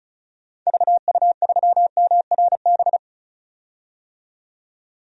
Morse code, heard as a keyed tone, about 36 wpm, 700 Hz. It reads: VU3MRB
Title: DXCC-100 Call Signs - 6 Characters - 35wpm